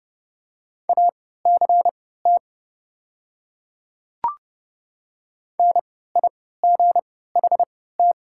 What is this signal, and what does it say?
Morse code, heard as a keyed tone, about 30 wpm, spaced as for 17 wpm, 700 Hz. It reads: ACT NIGHT
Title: 3x Character Spacing - Top 300 Words - 30wpm